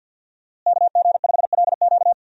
Morse code, heard as a keyed tone, about 52 wpm, 700 Hz. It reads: KG5LQ
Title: USA Call Signs - 5 Characters - Repeated 3x - 50wpm